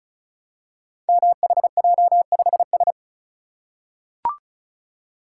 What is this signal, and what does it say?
Morse code, heard as a keyed tone, about 36 wpm, 700 Hz. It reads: MHJ5S
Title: Code Groups - 5 Letter:Numbers - 35wpm